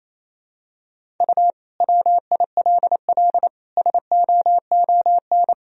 Morse code, heard as a keyed tone, about 28 wpm, 700 Hz. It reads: U WILL SOON
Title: Sentences from Top 400 Words and Abbreviations 28wpm